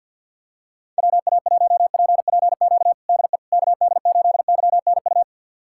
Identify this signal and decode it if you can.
Morse code, heard as a keyed tone, about 48 wpm, 690 Hz. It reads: WA1PPQ DE KD8YNU
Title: USA Callsign DE Callsign - 6 Characters - 50wpm